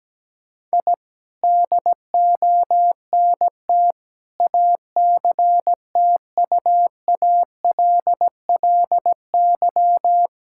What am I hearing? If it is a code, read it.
Morse code, heard as a keyed tone, about 17 wpm, 700 Hz. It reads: I DONT ACTUALLY
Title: Sentences from Top 1000 Words - Review Part 5 - 17wpm